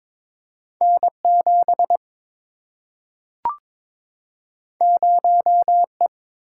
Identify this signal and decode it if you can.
Morse code, heard as a keyed tone, about 22 wpm, 700 Hz. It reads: N7 0E
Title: Single-Letter Single-Number 22wpm